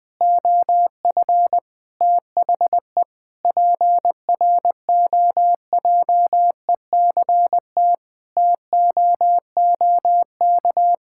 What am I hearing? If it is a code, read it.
Morse code, heard as a keyed tone, about 20 wpm, 700 Hz. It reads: OF THE PROJECT TOOK
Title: Sentences with 3-Word Phrases - Spoken Beginning - 20wpm